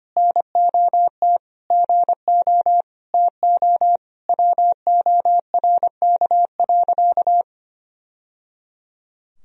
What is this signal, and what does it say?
Morse code, heard as a keyed tone, about 25 wpm, 700 Hz. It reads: NOT GO TO WORK.
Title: Sentences from Top 1000 Words - Review Part 2 - 25wpm